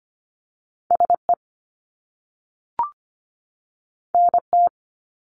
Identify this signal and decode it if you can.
Morse code, heard as a keyed tone, about 25 wpm, 700 Hz. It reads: SE NT
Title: Top Two Letter Combinations in Words 25wpm